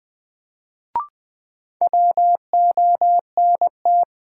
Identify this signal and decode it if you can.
Morse code, heard as a keyed tone, about 20 wpm, 700 Hz. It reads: WONT
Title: Sentences from Top 1000 Words - Review Part 6 - New Word Drill - 20wpm